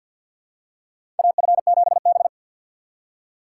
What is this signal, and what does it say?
Morse code, heard as a keyed tone, about 53 wpm, 700 Hz. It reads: AF7B
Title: USA Call Signs - 4 Characters - 6x Character Spacing - 50wpm